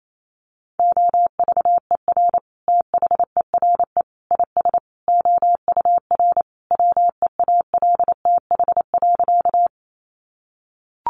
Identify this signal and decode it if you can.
Morse code, heard as a keyed tone, about 28 wpm, 700 Hz. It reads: OVER THERE IS OUR WEALTH.
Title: Sentences from Top 1200 Words 28wpm